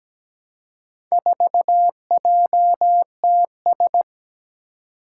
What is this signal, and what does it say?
Morse code, heard as a keyed tone, about 17 wpm, 700 Hz. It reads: 4JTS
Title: Four Challenging Letter-Number Combinations 17wpm